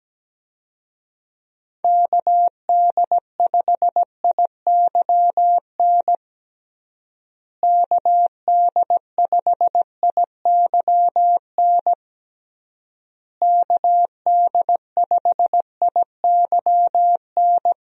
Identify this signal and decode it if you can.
Morse code, heard as a keyed tone, about 17 wpm, 700 Hz. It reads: KD5IYN KD5IYN KD5IYN
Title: USA Call Signs - 6 Characters - Repeated 3x - 17wpm